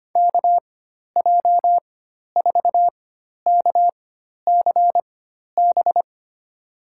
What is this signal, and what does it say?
Morse code, heard as a keyed tone, about 25 wpm, spaced as for 12 wpm, 700 Hz. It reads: KJ4KCB
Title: USA Call Signs - 6 Characters - 4x Character Spacing - 25wpm